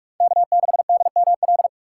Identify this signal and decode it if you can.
Morse code, heard as a keyed tone, about 45 wpm, 700 Hz. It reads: K6DML
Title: USA Call Signs - 5 Characters - 3x Character Spacing - 45wpm